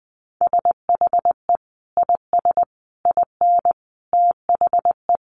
Morse code, heard as a keyed tone, about 20 wpm, 700 Hz. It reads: SHE IS IN THE
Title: Sentences from Top 600 Words 20wpm